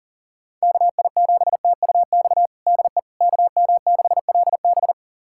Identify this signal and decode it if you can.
Morse code, heard as a keyed tone, about 37 wpm, 700 Hz. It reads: KI7TUX DE KM6LB
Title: USA Callsign DE Callsign 40wpm